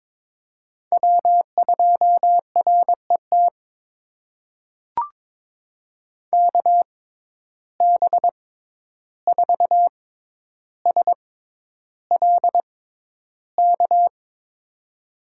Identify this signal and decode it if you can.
Morse code, heard as a keyed tone, about 22 wpm, spaced as for 11 wpm, 700 Hz. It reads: W2RET KB4SLK
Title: USA Call Signs - 6x Character Spacing - 22wpm